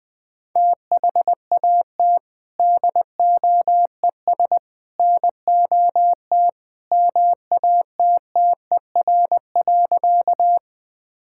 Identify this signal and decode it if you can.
Morse code, heard as a keyed tone, about 21 wpm, 700 Hz. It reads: THAT DOES NOT MATTER.
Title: Sentences with 3-Word Phrases - 20wpm